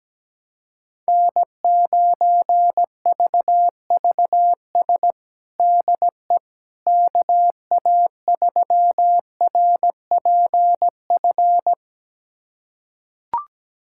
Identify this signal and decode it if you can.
Morse code, heard as a keyed tone, about 17 wpm, 700 Hz. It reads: N9VVS DE KA3RPF
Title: USA Callsign DE Callsign 17wpm